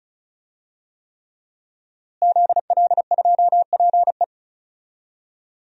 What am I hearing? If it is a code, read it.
Morse code, heard as a keyed tone, about 34 wpm, 700 Hz. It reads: ZL2PE
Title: DXCC-100 Call Signs - 5 Characters - Repeated 3x - 35wpm